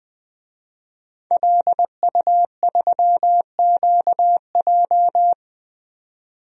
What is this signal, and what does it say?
Morse code, heard as a keyed tone, about 20 wpm, 700 Hz. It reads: LU3QJ
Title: DXCC-100 Call Signs - 5 Characters - 20wpm